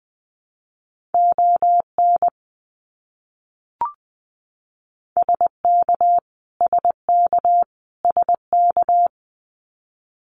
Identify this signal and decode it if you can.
Morse code, heard as a keyed tone, about 20 wpm, 700 Hz. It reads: ON SK SK SK
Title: Canadian Provinces and Territories - Repeated 3x - 20wpm